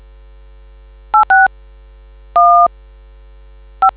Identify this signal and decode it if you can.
DTMF, heard as telephone touch tones, tones of about 150 ms. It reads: 86 1 5